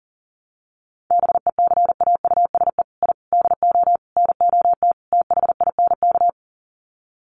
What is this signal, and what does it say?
Morse code, heard as a keyed tone, about 40 wpm, 700 Hz. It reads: BECAUSE I DO NOT THINK
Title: Sets of 5 Words Encore 40wpm